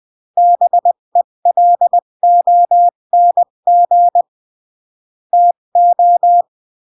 Morse code, heard as a keyed tone, about 20 wpm, 700 Hz. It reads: BELONG TO
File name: CW / jodorowsky / 82 commandments 20-15